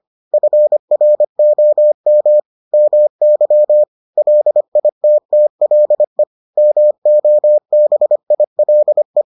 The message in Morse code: FROM MY LITTLE MOBILE

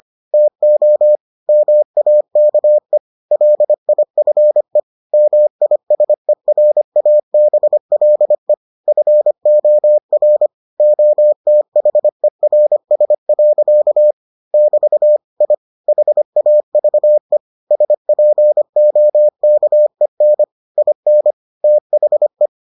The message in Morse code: TO MAKE LIFE MISERABLE FOR OTHERS. <BT> I HAVE SPOKEN IN THE

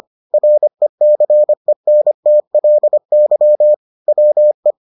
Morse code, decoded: RECENTLY WE

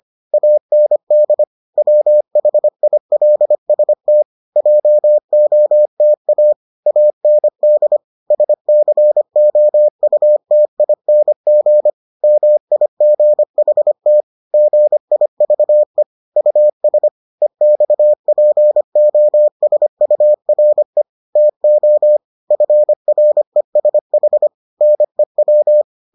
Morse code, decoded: AND WHILST JOTA AND SCOUTING MIGHT GIVE US EXPOSURE TO FRESH NEW

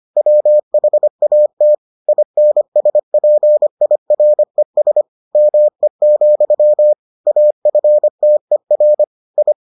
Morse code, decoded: WHAT INSPIRES ME, AFTER I

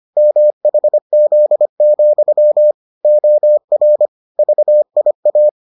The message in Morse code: MHZ, OR VIA

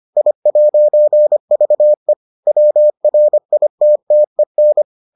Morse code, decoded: I'VE WRITTEN

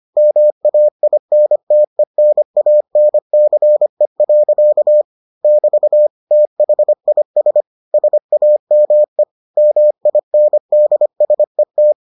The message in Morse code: MAINTENANCE. <BT> THIS SAME MINDSET

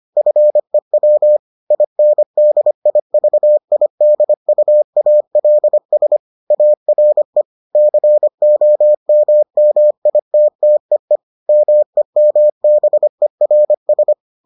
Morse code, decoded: FEW INDIVIDUALS ARE COMMITTEE MEMBERS

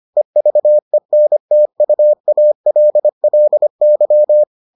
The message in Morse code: EVENTUALLY

